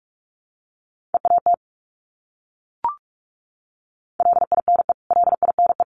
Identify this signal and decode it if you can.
Morse code, heard as a keyed tone, about 44 wpm, 700 Hz. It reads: EAT LINE LINE
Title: Top 500 Words Repeated 45wpm